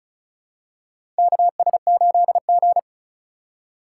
Morse code, heard as a keyed tone, about 34 wpm, 690 Hz. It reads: KS8G
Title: USA Call Signs - 4 Characters - 2x Character Spacing - 35wpm